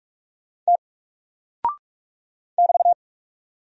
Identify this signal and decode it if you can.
Morse code, heard as a keyed tone, about 44 wpm, 700 Hz. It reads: T <BT>
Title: Single Character (Letter:Number:Punctuation:Prosign) 45wpm